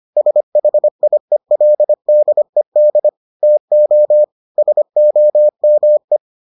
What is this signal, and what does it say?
Morse code, heard as a keyed tone, about 25 wpm, 600 Hz. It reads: SHIELDED TO SOME